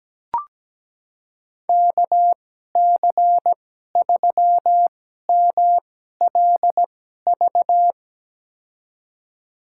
Morse code, heard as a keyed tone, about 17 wpm, spaced as for 12 wpm, 700 Hz. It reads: KC3MLV
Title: USA Call Signs - 6 Characters - 2x Character Spacing - 17wpm